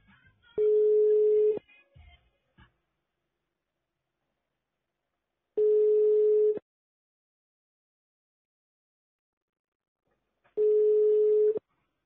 Ringback tone (Europe, 425 Hz).